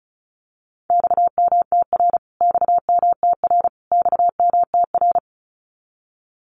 Morse code, heard as a keyed tone, about 34 wpm, 700 Hz. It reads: XMTR XMTR XMTR
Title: Abbreviations Repeated 3x - 35wpm